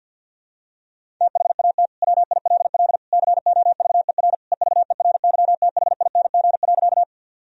Morse code, heard as a keyed tone, about 50 wpm, 700 Hz. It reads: THAT WILL COVER EVERYTHING.
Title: Sentences from Top 1000 Words - Review Part 1 - 50wpm